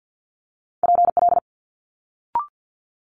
Morse code, heard as a keyed tone, about 50 wpm, 700 Hz. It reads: PL
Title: Multi-letter Phonemes Quick Fire 50wpm